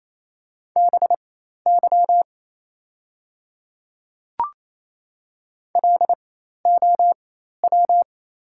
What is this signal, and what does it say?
Morse code, heard as a keyed tone, about 27 wpm, spaced as for 13 wpm, 690 Hz. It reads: BY LOW
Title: 4x Character Spacing - Top 200 Words - 28wpm